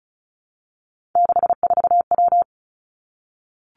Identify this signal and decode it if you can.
Morse code, heard as a keyed tone, about 36 wpm, 700 Hz. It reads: 64W
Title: Three Challenging Letter-Number Combinations 35wpm